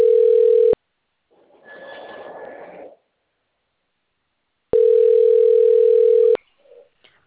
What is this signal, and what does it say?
Ringback tone (North America), 440 + 480 Hz.
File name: out- - - - - .3320